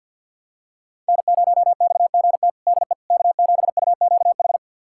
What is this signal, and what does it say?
Morse code, heard as a keyed tone, about 52 wpm, 700 Hz. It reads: N0XGT DE K7UQH
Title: USA Callsign DE Callsign - 5 Characters - 50wpm